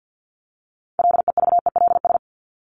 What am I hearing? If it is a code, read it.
Morse code, heard as a keyed tone, about 50 wpm, 700 Hz. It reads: LEVELS
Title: ES:S Suffix - Top 100 Words with Base Words 50wpm